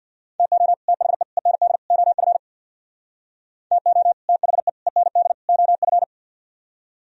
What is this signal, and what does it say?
Morse code, heard as a keyed tone, about 64 wpm, 700 Hz. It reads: TO THE END OF TO THE END OF